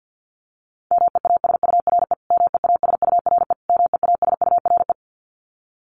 Morse code, heard as a keyed tone, about 50 wpm, 700 Hz. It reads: MEASURE MEASURE MEASURE
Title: Top 500 Words Repeated 50wpm